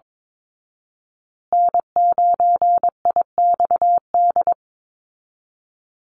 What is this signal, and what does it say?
Morse code, heard as a keyed tone, about 22 wpm, 700 Hz. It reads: N9IXD
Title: Call Signs Repeated 2x - 22wpm